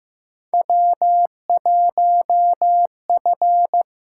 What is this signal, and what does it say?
Morse code, heard as a keyed tone, about 15 wpm, 700 Hz. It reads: W1F
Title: USA Call Signs - 4 Characters - 8x Character Spacing - 15wpm